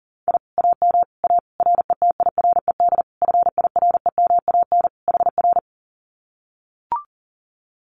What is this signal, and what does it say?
Morse code, heard as a keyed tone, about 40 wpm, 700 Hz. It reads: I AM A RETIRED FIREMAN HR
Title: Common QSO Phrases Encore 40wpm